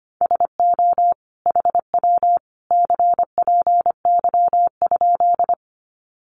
Morse code, heard as a keyed tone, about 25 wpm, 700 Hz. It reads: SO HW CPY?
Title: Common QSO Phrases 25wpm